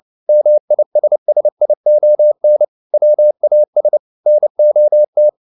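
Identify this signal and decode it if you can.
Morse code, heard as a keyed tone, about 29 wpm, 600 Hz. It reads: MISSION WAS NOT
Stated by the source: Morse code at 29 wpm, sending Ken Follett's novel Never in English, from CW med Gnister.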